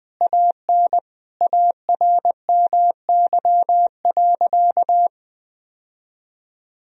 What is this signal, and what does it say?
Morse code, heard as a keyed tone, about 21 wpm, 700 Hz. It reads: AN ARMY.
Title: Sentences from Top 1000 Words - Review Part 1 - 20wpm